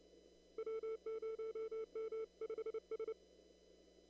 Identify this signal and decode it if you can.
Morse code, heard as a keyed tone, about 29 wpm, 440 Hz. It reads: W0M5S
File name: CW / kt5j wpx cw-op-k5pi